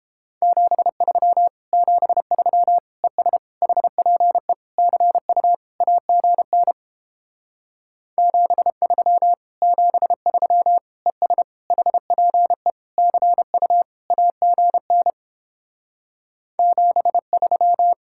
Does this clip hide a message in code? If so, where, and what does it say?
Yes: Morse code from 0.4 to 17.9 s, heard as a keyed tone, about 31 wpm, 700 Hz. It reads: 73 73 ES HPE CU AGN 73 73 ES HPE CU AGN 73